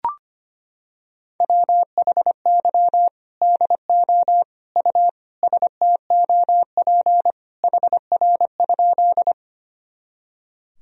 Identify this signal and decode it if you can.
Morse code, heard as a keyed tone, about 25 wpm, 700 Hz. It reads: WHY DO U STOP HR?
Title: Sentences from Top 300 Words and Abbreviations 25wpm